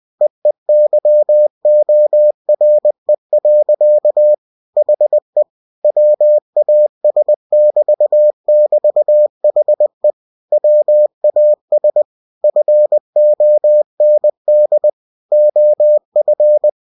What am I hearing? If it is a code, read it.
Morse code, heard as a keyed tone, about 20 wpm, 600 Hz. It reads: EEYORE. HE WAS<BT><BT>HE WAS FOND OF
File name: cw-books / winnie-the-pooh / winnie-20wpm-page-0032